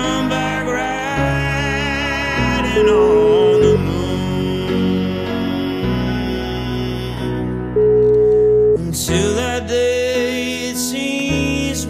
Ringback tone (Europe, 425 Hz), with noise.